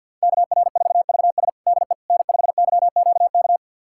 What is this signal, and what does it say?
Morse code, heard as a keyed tone, about 48 wpm, 700 Hz. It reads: KR4VS DE N5YQK
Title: USA Callsign DE Callsign - 5 Characters - 50wpm